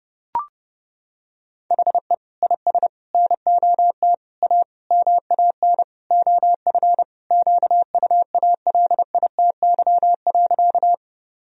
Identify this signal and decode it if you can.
Morse code, heard as a keyed tone, about 30 wpm, 700 Hz. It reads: HE IS NOT A MAN OF QUALITY.
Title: Sentences from Top 1000 Words - Review Part 4 - 30wpm